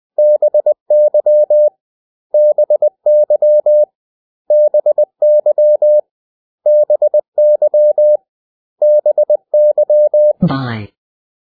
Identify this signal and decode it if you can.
Morse code, heard as a keyed tone, about 20 wpm, 600 Hz. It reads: BY BY BY BY BY
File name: cw / 100 Common English Words